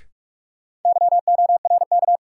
Morse code, heard as a keyed tone, about 46 wpm, 700 Hz. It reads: YORK
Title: Sets of 2 Words - New Word Drill - 45wpm